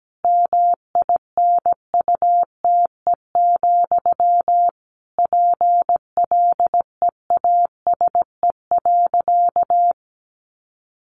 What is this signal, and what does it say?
Morse code, heard as a keyed tone, about 17 wpm, 700 Hz. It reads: MINUTE, PLEASE.
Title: Sentences from Top 1500 Words 17wpm